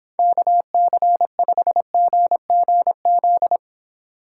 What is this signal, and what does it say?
Morse code, heard as a keyed tone, about 26 wpm, 700 Hz. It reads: KC5GGZ